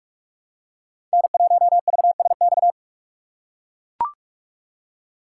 Morse code, heard as a keyed tone, about 44 wpm, 700 Hz. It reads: N1VSX
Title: USA Call Signs - 2x Character Spacing - 45wpm